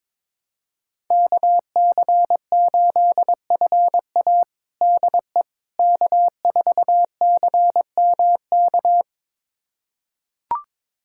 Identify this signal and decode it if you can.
Morse code, heard as a keyed tone, about 21 wpm, 700 Hz. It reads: KC8FA DE K4CMK
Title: USA Callsign DE Callsign - 5 Characters - 22wpm